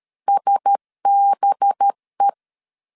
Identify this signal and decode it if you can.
Morse code, heard as a keyed tone, about 13 wpm, 800 Hz. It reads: SBE
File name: CW / DET / 1 9 1500C S B 12wpm 3819